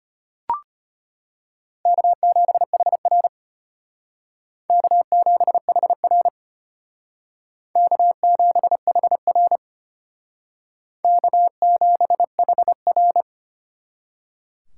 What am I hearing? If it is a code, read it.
Morse code, heard as a keyed tone, about 32 wpm, 700 Hz. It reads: K7HR K7HR K7HR K7HR